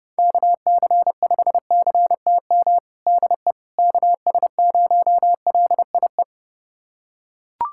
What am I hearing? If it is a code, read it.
Morse code, heard as a keyed tone, about 30 wpm, 700 Hz. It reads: KC5CTM DE KS0LIE
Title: USA Callsign DE Callsign - 6 Characters - 30wpm